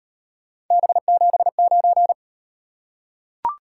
Morse code, heard as a keyed tone, about 38 wpm, 700 Hz. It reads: B79